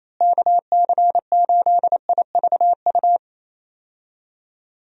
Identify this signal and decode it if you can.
Morse code, heard as a keyed tone, about 27 wpm, 700 Hz. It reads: KC8IVU
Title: USA Call Signs - 6 Characters - Repeated 3x - 28wpm